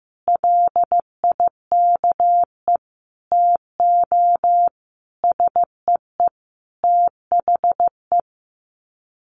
Morse code, heard as a keyed tone, about 15 wpm, 700 Hz. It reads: LIKE TO SEE THE
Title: Sets of 5 Words Encore2 15wpm